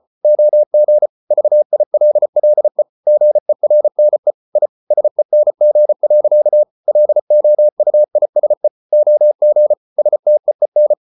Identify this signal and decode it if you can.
Morse code, heard as a keyed tone, about 34 wpm, 600 Hz. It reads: OG VILLE GERNE I SENG. LOUISE OG STEEN
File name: Ferie QRQ 34wpm